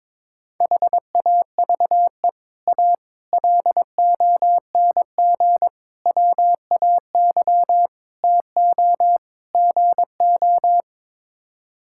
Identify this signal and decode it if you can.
Morse code, heard as a keyed tone, about 22 wpm, 700 Hz. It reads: HAVE A LONG WAY TO GO
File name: practice / 22wpm / Sets of 6 Words - 22wpm